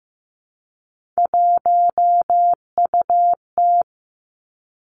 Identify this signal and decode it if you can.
Morse code, heard as a keyed tone, about 15 wpm, 700 Hz. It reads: 1UT
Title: Three Challenging Letter-Number Combinations 15wpm